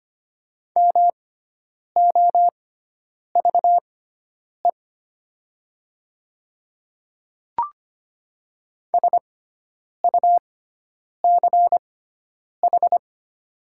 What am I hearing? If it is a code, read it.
Morse code, heard as a keyed tone, about 25 wpm, spaced as for 9 wpm, 700 Hz. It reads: MOVE SUCH
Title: 6x Character Spacing - Top 300 Words - 25wpm